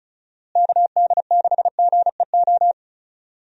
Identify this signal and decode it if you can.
Morse code, heard as a keyed tone, about 36 wpm, 700 Hz. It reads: KD6GEO
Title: USA Call Signs - 6 Characters - Repeated 2x - 35wpm